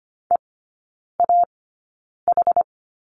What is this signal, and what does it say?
Morse code, heard as a keyed tone, about 25 wpm, spaced as for 9 wpm, 700 Hz. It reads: EAH
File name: course / 25:10wpm / 093 Lesson 16e - Letter c - Three Characters 25wpm